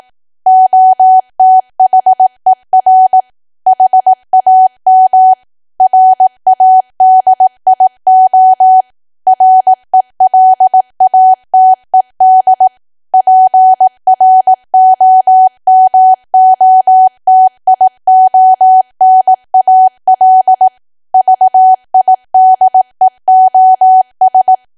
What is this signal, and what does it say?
Morse code, heard as a keyed tone, about 18 wpm, 750 Hz. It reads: OTHER HAM RADIO RELATED PROMOTIONAL VIDEOS
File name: Archive / 18 WPM / 18WPM